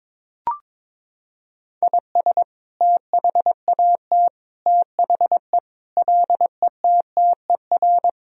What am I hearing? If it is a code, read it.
Morse code, heard as a keyed tone, about 22 wpm, 700 Hz. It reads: IS THAT THE LETTER
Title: Sentences from Top 1000 Words - Review Part 2 - 22wpm